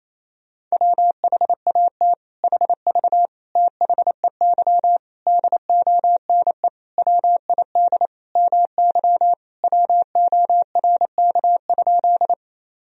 Morse code, heard as a keyed tone, about 28 wpm, 700 Hz. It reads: WHAT HV THEY DONE WID MY WORK?
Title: Sentences from Top 500 Words and Abbreviations 28wpm